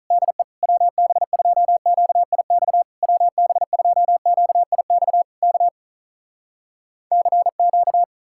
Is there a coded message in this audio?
Yes: Morse code from 0.1 to 8.0 s, heard as a keyed tone, about 46 wpm, 700 Hz. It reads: DE WB2QIX WB2QIX K CQ